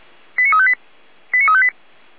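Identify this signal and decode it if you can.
Five-tone selective call (ZVEI), heard as quick runs of five tones, tones of about 70 ms. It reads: 89358 89358
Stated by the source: ZVEI five-tone call received on an SDR, fire brigade feuerwehr-orlamuende.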